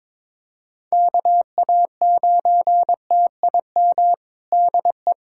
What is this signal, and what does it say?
Morse code, heard as a keyed tone, about 22 wpm, 700 Hz. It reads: KA9TIM DE
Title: USA Callsign DE Callsign - 6 Characters - 22wpm